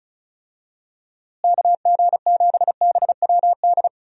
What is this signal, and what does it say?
Morse code, heard as a keyed tone, about 36 wpm, 700 Hz. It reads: KG7BWD
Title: USA Call Signs - 6 Characters - Repeated 3x - 35wpm